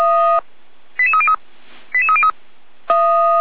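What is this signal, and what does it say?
Five-tone selective call (ZVEI), heard as quick runs of five tones, tones of about 70 ms. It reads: 88303 88303
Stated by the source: ZVEI five-tone call received on an SDR, from audio.f85360.feuerwehr-orlamuende.org (Feuerwehr Orlamünde fire brigade).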